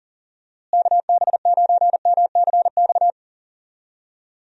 Morse code, heard as a keyed tone, about 40 wpm, 700 Hz. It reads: KB9MCX
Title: USA Call Signs - 6 Characters - Repeated 3x - 40wpm